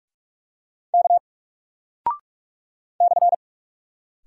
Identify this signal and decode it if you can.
Morse code, heard as a keyed tone, about 44 wpm, 700 Hz. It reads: K /
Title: Single Character (Letter:Number:Punctuation:Prosign) Alt 45wpm